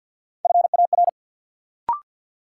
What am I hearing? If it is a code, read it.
Morse code, heard as a keyed tone, about 53 wpm, 700 Hz. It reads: FAR
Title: Sentences from Top 300 Words - New Word Drill - 50wpm